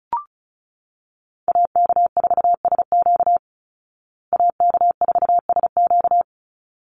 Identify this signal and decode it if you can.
Morse code, heard as a keyed tone, about 34 wpm, 700 Hz. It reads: AK4SQ AK4SQ
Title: Call Signs Repeated 3x - 35wpm